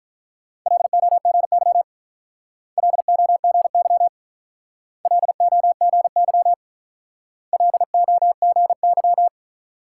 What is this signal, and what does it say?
Morse code, heard as a keyed tone, about 44 wpm, 700 Hz. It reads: LOGY LOGY LOGY LOGY
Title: Speed-Racing Multi-letter Phonemes - 35wpm